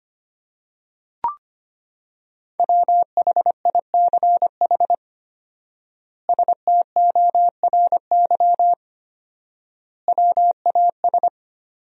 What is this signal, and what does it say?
Morse code, heard as a keyed tone, about 25 wpm, 700 Hz. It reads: WHICH STORY WAS
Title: 4x Word-Spacing - Sentences from Top 300 Words - 25wpm